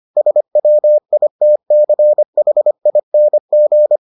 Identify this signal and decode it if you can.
Morse code, heard as a keyed tone, about 25 wpm, 600 Hz. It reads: SWITCHING